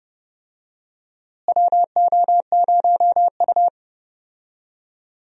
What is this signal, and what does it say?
Morse code, heard as a keyed tone, about 30 wpm, 700 Hz. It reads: WO0U
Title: USA Call Signs - 4 Characters - Repeated 3x - 30wpm